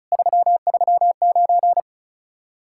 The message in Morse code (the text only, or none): 339